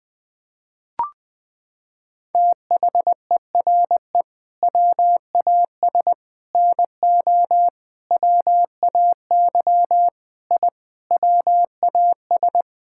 Morse code, heard as a keyed tone, about 20 wpm, 700 Hz. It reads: THERE WAS NO WAY I WAS
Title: Sets of 8 Words - 20wpm